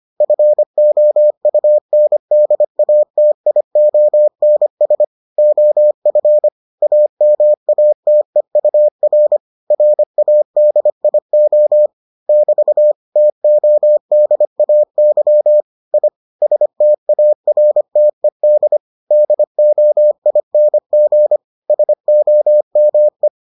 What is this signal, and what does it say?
Morse code, heard as a keyed tone, about 25 wpm, 600 Hz. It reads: FOUNDATIONS OF AMATEUR RADIO <BT> TODAY I STARTED DOING SOME